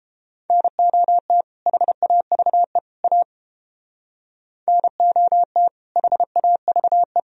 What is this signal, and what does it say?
Morse code, heard as a keyed tone, about 31 wpm, 700 Hz. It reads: NOT HAVE A NOT HAVE